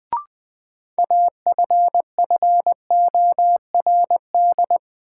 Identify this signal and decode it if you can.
Morse code, heard as a keyed tone, about 20 wpm, 700 Hz. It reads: AFFORD
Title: Sentences from Top 1100 Words - New Word Drill Alt - 20wpm